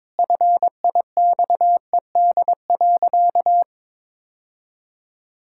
Morse code, heard as a keyed tone, about 22 wpm, 700 Hz. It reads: FIXED.
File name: practice / 22wpm / Sentences with 3-Word Phrases - 22wpm